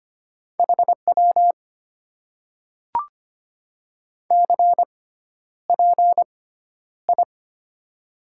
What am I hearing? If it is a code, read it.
Morse code, heard as a keyed tone, about 25 wpm, spaced as for 11 wpm, 700 Hz. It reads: HW CPI
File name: Top 90 QSO Elements - 6x Character Spacing - 25wpm